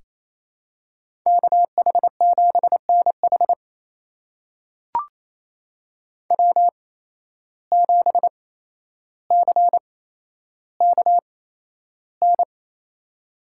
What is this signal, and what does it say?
Morse code, heard as a keyed tone, about 28 wpm, spaced as for 12 wpm, 700 Hz. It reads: KH7NH W7CKN
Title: USA Call Signs - 5 Characters - 8x Character Spacing - 28wpm